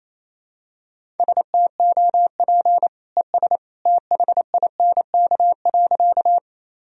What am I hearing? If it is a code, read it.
Morse code, heard as a keyed tone, about 28 wpm, 700 Hz. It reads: STOP ES THINK.
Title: Sentences from Top 300 Words and Abbreviations 28wpm